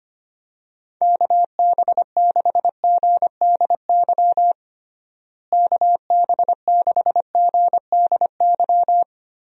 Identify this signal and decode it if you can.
Morse code, heard as a keyed tone, about 25 wpm, 700 Hz. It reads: KB6GDY KB6GDY
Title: USA Call Signs - 6 Characters - Repeated 3x - 25wpm